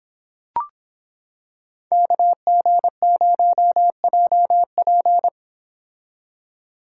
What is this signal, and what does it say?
Morse code, heard as a keyed tone, about 26 wpm, 700 Hz. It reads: KG0JP